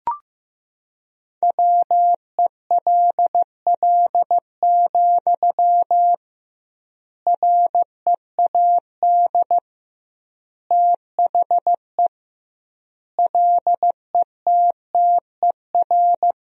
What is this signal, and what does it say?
Morse code, heard as a keyed tone, about 15 wpm, 700 Hz. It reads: WELL, READ THE LETTER
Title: 2x Word-Spacing - Sentences from Top 300 Words - 15wpm